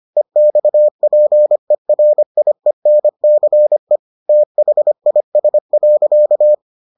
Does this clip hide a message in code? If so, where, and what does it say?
Yes: Morse code from 0.2 to 6.5 s, heard as a keyed tone, about 25 wpm, 600 Hz. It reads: EXPERIENCE THIS.